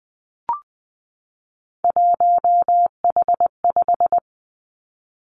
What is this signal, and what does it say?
Morse code, heard as a keyed tone, about 20 wpm, 700 Hz. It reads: 1H5
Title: Three Challenging Letter-Number Combinations 20wpm